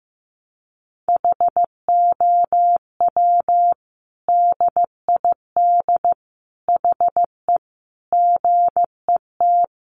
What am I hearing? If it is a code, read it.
Morse code, heard as a keyed tone, about 15 wpm, 700 Hz. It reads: HOW DID HE GET
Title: Sentences from Top 1100 Words 15wpm